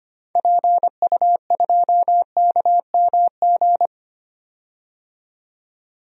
Morse code, heard as a keyed tone, about 25 wpm, 700 Hz. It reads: PU2KMG
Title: DXCC-100 Call Signs - 6 Characters - 25wpm